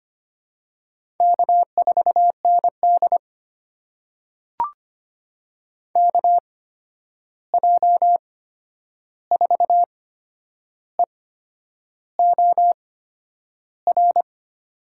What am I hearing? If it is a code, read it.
Morse code, heard as a keyed tone, about 25 wpm, spaced as for 9 wpm, 700 Hz. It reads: K4ND KJ4EOR
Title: USA Call Signs - 8x Character Spacing - 25wpm